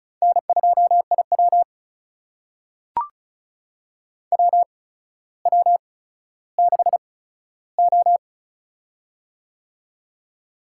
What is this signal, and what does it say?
Morse code, heard as a keyed tone, about 36 wpm, spaced as for 15 wpm, 700 Hz. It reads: N2IW WW6O
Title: USA Call Signs - 4 Characters - 8x Character Spacing - 35wpm